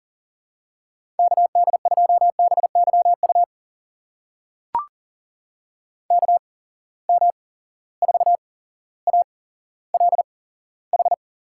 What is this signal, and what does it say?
Morse code, heard as a keyed tone, about 40 wpm, spaced as for 17 wpm, 700 Hz. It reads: KD2BYU KM4ALH
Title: USA Call Signs - 6 Characters - 8x Character Spacing - 40wpm